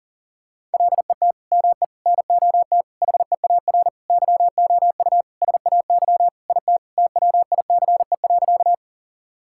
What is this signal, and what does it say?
Morse code, heard as a keyed tone, about 40 wpm, 700 Hz. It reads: LET ME NOT HEAR YOU SAY IT TWICE.
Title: Sentences from Top 1600 Words 40wpm